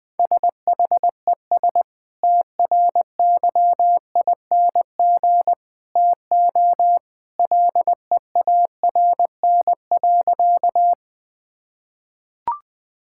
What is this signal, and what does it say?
Morse code, heard as a keyed tone, about 21 wpm, 700 Hz. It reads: SHES TRYING TO LEARN.
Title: Sentences with 2-Word Phrases - 20wpm